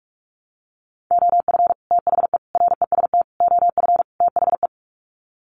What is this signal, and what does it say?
Morse code, heard as a keyed tone, about 44 wpm, 700 Hz. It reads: OF THE REST OF THE
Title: Sets of 5 Words 45wpm